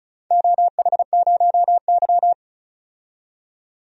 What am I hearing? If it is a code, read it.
Morse code, heard as a keyed tone, about 34 wpm, 700 Hz. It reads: OH0Y
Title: DXCC-100 Call Signs - 4 Characters - 35wpm